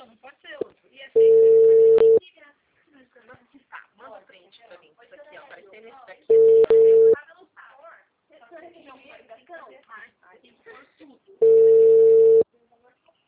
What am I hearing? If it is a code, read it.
Ringback tone (China, 450 Hz), with speech.
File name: out- -2013- - - .48326